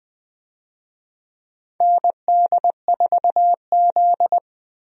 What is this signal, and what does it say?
Morse code, heard as a keyed tone, about 21 wpm, 700 Hz. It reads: ND4Z
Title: USA Call Signs - 4 Characters - 20wpm